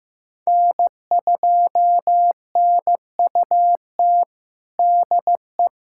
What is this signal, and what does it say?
Morse code, heard as a keyed tone, about 15 wpm, 700 Hz. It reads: N2NUT DE
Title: USA Callsign DE Callsign - 5 Characters - 15wpm